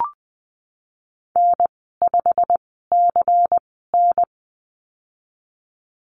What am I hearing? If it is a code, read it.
Morse code, heard as a keyed tone, about 20 wpm, spaced as for 15 wpm, 700 Hz. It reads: N5CN
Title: 173 Lesson 28g - Number 9 - Call Signs 20wpm